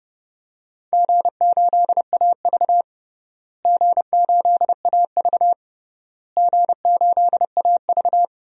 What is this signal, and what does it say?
Morse code, heard as a keyed tone, about 30 wpm, 700 Hz. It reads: G8AV G8AV G8AV
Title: DXCC-100 Call Signs - 4 Characters - Repeated 3x - 30wpm